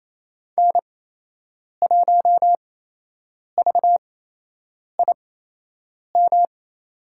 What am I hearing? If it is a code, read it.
Morse code, heard as a keyed tone, about 29 wpm, spaced as for 8 wpm, 700 Hz. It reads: N1VIM
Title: USA Call Signs - 8x Character Spacing - 28wpm